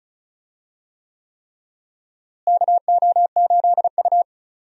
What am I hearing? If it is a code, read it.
Morse code, heard as a keyed tone, about 36 wpm, 700 Hz. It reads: KO8U
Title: USA Call Signs - 4 Characters - Repeated 3x - 35wpm